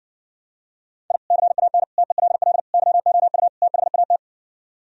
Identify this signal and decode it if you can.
Morse code, heard as a keyed tone, about 70 wpm, 700 Hz. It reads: I CAN TELL YOU THAT